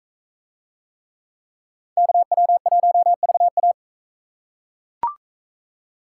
Morse code, heard as a keyed tone, about 41 wpm, 700 Hz. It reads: KW1VA